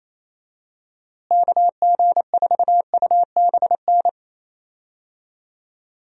Morse code, heard as a keyed tone, about 28 wpm, 700 Hz. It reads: KG4UBN
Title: USA Call Signs - 6 Characters - Repeated 3x - 28wpm